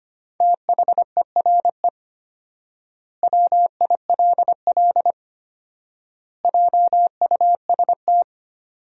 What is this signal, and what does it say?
Morse code, heard as a keyed tone, about 25 wpm, 700 Hz. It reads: THERE WILL JUST
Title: 4x Word-Spacing - Sentences from Top 200 Words - 25wpm